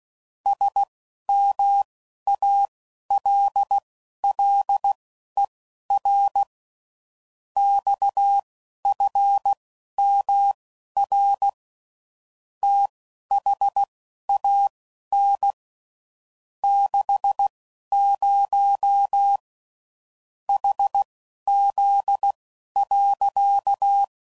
Morse code, heard as a keyed tone, about 16 wpm, spaced as for 11 wpm, 800 Hz. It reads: SMALLER XFMR THAN 60 HZ.
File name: CW Code Groups @10WPM / Technical Subjects / SwPwrSup